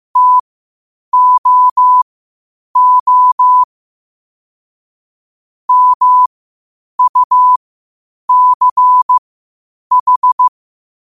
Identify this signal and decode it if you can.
Morse code, heard as a keyed tone, about 15 wpm, spaced as for 8 wpm, 1000 Hz. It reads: TOO MUCH
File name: qotdfiles / 8wpm